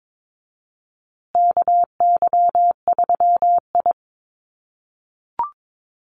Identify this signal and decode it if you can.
Morse code, heard as a keyed tone, about 22 wpm, 700 Hz. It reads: KY3I
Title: USA Call Signs Repeated 2x 22wpm